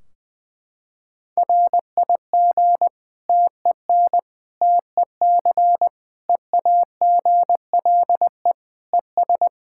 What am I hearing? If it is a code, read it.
Morse code, heard as a keyed tone, about 20 wpm, 700 Hz. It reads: RIG TEN TEC EAGLE ES
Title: K1USN SST QSO Practice 20wpm